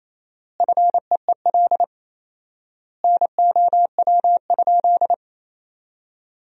Morse code, heard as a keyed tone, about 27 wpm, 700 Hz. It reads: FEEL NOW?
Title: 4x Word-Spacing - Sentences from Top 400 Words - 28wpm